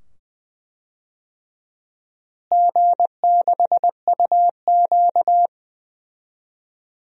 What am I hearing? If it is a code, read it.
Morse code, heard as a keyed tone, about 20 wpm, 700 Hz. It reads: G6UQ
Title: DXCC-100 Call Signs - 4 Characters - Repeated 2x - 20wpm